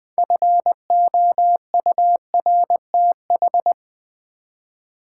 Morse code, heard as a keyed tone, about 20 wpm, 700 Hz. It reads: FOURTH
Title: 6 Letter Words - First 3 Letters Given - 20wpm